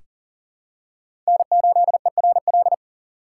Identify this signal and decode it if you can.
Morse code, heard as a keyed tone, about 38 wpm, 700 Hz. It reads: N8ERL
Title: USA Call Signs - 5 Characters - 6x Character Spacing - 40wpm